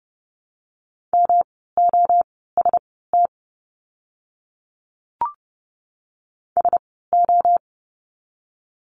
Morse code, heard as a keyed tone, about 30 wpm, spaced as for 17 wpm, 700 Hz. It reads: MOST SO
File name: course / 30:10wpm / 237 Lesson 38f - Letter q - Words 30wpm